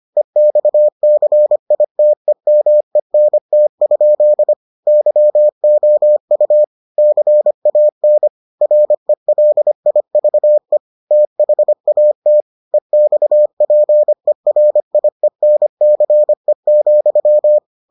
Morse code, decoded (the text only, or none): EXCITEMENT? YOU CAN RELIVE THAT EXPERIENCE,